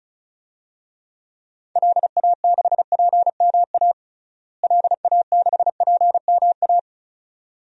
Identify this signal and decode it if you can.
Morse code, heard as a keyed tone, about 34 wpm, 700 Hz. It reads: LA6PMA LA6PMA
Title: DXCC-100 Call Signs - 6 Characters - Repeated 2x - 35wpm